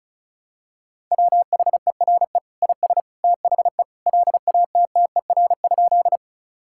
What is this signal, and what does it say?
Morse code, heard as a keyed tone, about 36 wpm, 700 Hz. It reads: WHERE IS THE LATTER?
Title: Sentences from Top 1600 Words 35wpm